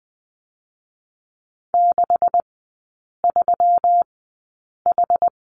Morse code, heard as a keyed tone, about 20 wpm, spaced as for 8 wpm, 700 Hz. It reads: 63H
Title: 133 Lesson 22e - Question Mark - Three Characters 20wpm